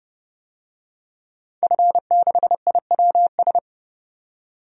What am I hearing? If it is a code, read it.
Morse code, heard as a keyed tone, about 30 wpm, 700 Hz. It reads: F6IWS
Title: DXCC-100 Call Signs - 5 Characters - 30wpm